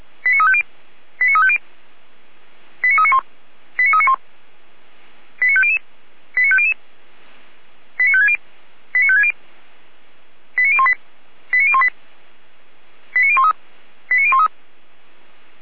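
Five-tone selective call (ZVEI), heard as quick runs of five tones, tones of about 70 ms. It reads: 89350 89350 89391 89391 89500 89500 89560 89560 89018 89018 89013 89013
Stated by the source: ZVEI five-tone selective call received on SDR.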